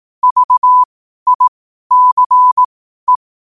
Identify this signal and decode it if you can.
Morse code, heard as a keyed tone, about 18 wpm, spaced as for 12 wpm, 1000 Hz. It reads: VICE
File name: qotdfiles / 12wpm